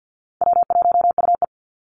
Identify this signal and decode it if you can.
Morse code, heard as a keyed tone, about 50 wpm, 700 Hz. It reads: W1UE
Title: Easy Call Signs Repeated 2x - 50wpm